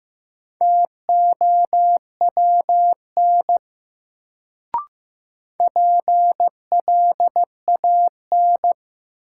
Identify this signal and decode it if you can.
Morse code, heard as a keyed tone, about 15 wpm, 700 Hz. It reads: TOWN PLAN
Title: Sentences from Top 500 Words - New Word Drill - 15wpm